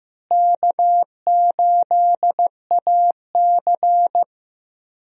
Morse code, heard as a keyed tone, about 15 wpm, 700 Hz. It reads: K8AC
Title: USA Call Signs - 4 Characters - Repeated 2x - 15wpm